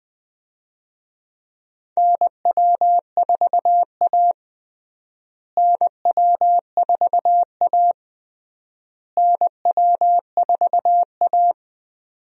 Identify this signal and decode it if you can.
Morse code, heard as a keyed tone, about 20 wpm, 690 Hz. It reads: NW4A NW4A NW4A
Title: USA Call Signs - 4 Characters - Repeated 3x - 20wpm